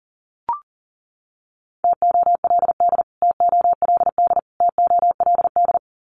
Morse code, heard as a keyed tone, about 37 wpm, 700 Hz. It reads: TOLD TOLD TOLD
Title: Top 500 Words Repeated 40wpm